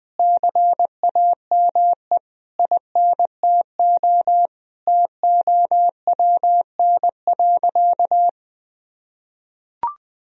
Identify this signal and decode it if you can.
Morse code, heard as a keyed tone, about 20 wpm, 700 Hz. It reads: CAME INTO TOWN.
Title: Sentences from Top 1000 Words - Review Part 2 - 20wpm